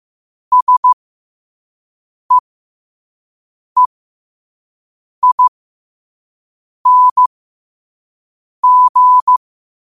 Morse code, heard as a keyed tone, about 15 wpm, spaced as for 5 wpm, 1000 Hz. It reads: SEEING
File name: qotdfiles / 5wpm